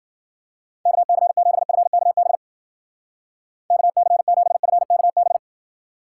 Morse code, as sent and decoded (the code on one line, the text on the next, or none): -.- -.-. --... ..-. -.- -... / -.- -.-. --... ..-. -.- -...
KC7FKB KC7FKB